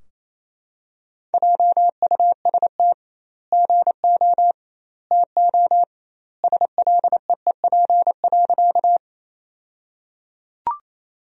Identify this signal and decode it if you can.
Morse code, heard as a keyed tone, about 28 wpm, 700 Hz. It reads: JUST GO TO SLEEP.
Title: 2x Word-Spacing - Sentences from Top 500 Words - 28wpm